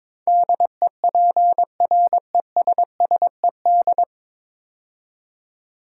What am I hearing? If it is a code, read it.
Morse code, heard as a keyed tone, about 22 wpm, 700 Hz. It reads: DEPRESSED
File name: practice / 22wpm / 9 Letter Words - First 3 Letters Given - 22wpm